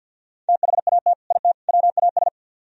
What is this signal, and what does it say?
Morse code, heard as a keyed tone, about 60 wpm, 700 Hz. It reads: THAT IT WAS
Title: Speed-Racing - Sets of 3 Words 50wpm